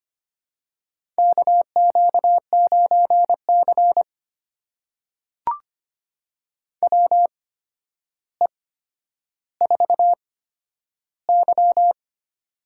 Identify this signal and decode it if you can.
Morse code, heard as a keyed tone, about 25 wpm, spaced as for 11 wpm, 700 Hz. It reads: KQ9C WE4Y
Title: USA Call Signs - 4 Characters - 8x Character Spacing - 25wpm